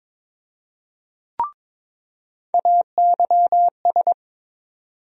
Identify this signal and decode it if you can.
Morse code, heard as a keyed tone, about 22 wpm, 700 Hz. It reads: AYS
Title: USA Call Sign Postfixes - 3 Letters 22wpm